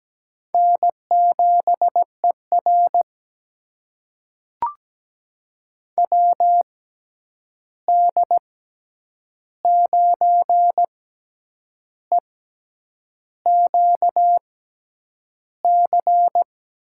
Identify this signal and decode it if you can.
Morse code, heard as a keyed tone, about 17 wpm, spaced as for 8 wpm, 700 Hz. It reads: N7ER WD9EQC
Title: USA Call Signs - 6x Character Spacing - 17wpm